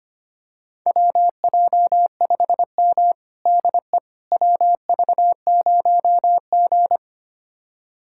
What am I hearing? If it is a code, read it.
Morse code, heard as a keyed tone, about 25 wpm, 700 Hz. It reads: WJ5M DE WV0G
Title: USA Callsign DE Callsign - 4 Characters - 25wpm